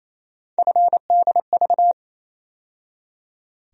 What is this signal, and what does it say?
Morse code, heard as a keyed tone, about 28 wpm, 700 Hz. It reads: FDV
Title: USA Call Sign Postfixes - 3 Letters 28wpm